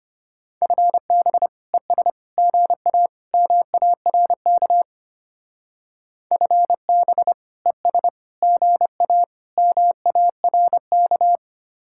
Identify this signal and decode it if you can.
Morse code, heard as a keyed tone, about 33 wpm, 700 Hz. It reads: FB ES GA MARK FB ES GA MARK